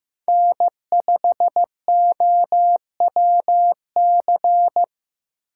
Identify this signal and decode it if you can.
Morse code, heard as a keyed tone, about 18 wpm, 700 Hz. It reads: N5OWC
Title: Speed Racing - USA 1x3 Call Signs 15wpm